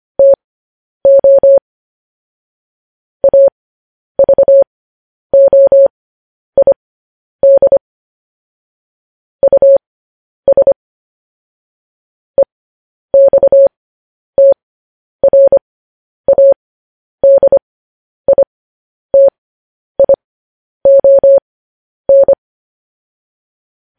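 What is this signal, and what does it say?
Morse code, heard as a keyed tone, about 25 wpm, spaced as for 10 wpm, 550 Hz. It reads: TO AVOID US EXTRADITION
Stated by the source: Morse code news headlines.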